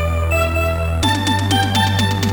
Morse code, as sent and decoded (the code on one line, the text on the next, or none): none